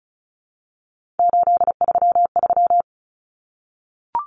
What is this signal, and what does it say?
Morse code, heard as a keyed tone, about 34 wpm, 700 Hz. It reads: 833
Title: Numbers - 3-Digit 35wpm